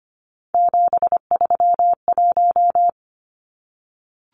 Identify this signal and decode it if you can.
Morse code, heard as a keyed tone, about 25 wpm, 700 Hz. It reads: 731
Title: Numbers - 3-Digit 25wpm